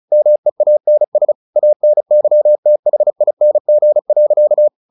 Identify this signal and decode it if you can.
Morse code, heard as a keyed tone, about 34 wpm, 600 Hz. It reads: MEANS ANYTHING.
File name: the old man and the sea 2 35wpm